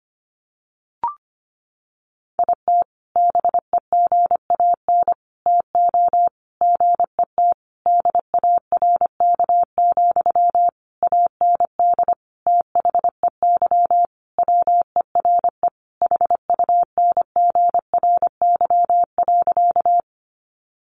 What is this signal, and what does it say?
Morse code, heard as a keyed tone, about 25 wpm, 700 Hz. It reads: IT BEGAN TO GET DARK, AND THEY WERE HUNGRY.
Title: Sentences from Top 1400 Words 25wpm